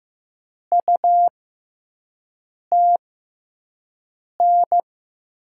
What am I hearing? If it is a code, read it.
Morse code, heard as a keyed tone, about 15 wpm, spaced as for 5 wpm, 700 Hz. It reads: UTN
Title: USA Call Signs - 6 Characters - 6x Character Spacing - 15wpm